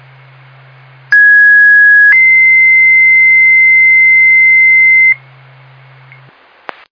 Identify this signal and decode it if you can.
Two-tone page, heard as a one-second tone, then a three-second tone, about 1650 then 2050 Hz. It is heard from 1.1 to 5.1 s.